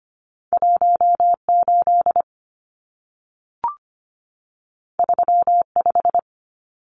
Morse code, heard as a keyed tone, about 25 wpm, 700 Hz. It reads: 18 35
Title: Numbers - 2-Digit 25wpm